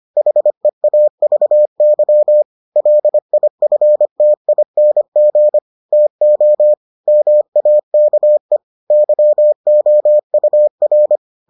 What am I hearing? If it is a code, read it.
Morse code, heard as a keyed tone, about 25 wpm, 600 Hz. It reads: HEAVY LIFTING TO MAKE YOUR